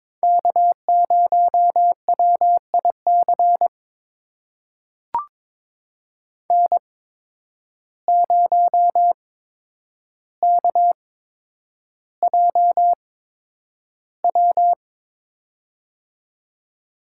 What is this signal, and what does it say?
Morse code, heard as a keyed tone, about 22 wpm, spaced as for 9 wpm, 700 Hz. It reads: K0WIC N0KJW
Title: USA Call Signs - 5 Characters - 8x Character Spacing - 22wpm